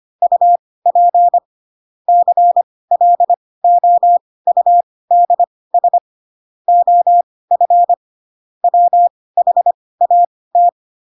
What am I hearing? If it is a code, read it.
Morse code, heard as a keyed tone, about 25 wpm, spaced as for 18 wpm, 700 Hz. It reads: UP CLOUDS OF WHAT
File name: Pinocchio ch3 12 25 ik1hge